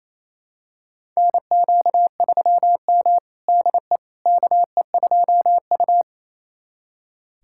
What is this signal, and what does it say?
Morse code, heard as a keyed tone, about 28 wpm, 700 Hz. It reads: NQ3M DE KE2U
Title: USA Callsign DE Callsign - 4 Characters - 28wpm